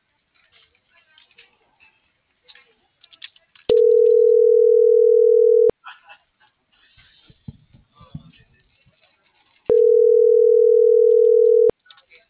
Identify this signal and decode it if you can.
Ringback tone (North America), 440 + 480 Hz.